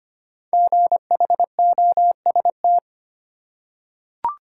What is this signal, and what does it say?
Morse code, heard as a keyed tone, about 25 wpm, 700 Hz. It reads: GHOST
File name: 5 Letter Words - First 3 Letters Given - 25wpm